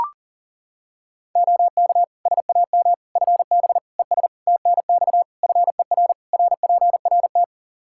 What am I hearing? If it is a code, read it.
Morse code, heard as a keyed tone, about 40 wpm, 700 Hz. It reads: OK SAM FB ES TNX FER RPRT
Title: Practice QSO - Exchange 2 - Info - 40wpm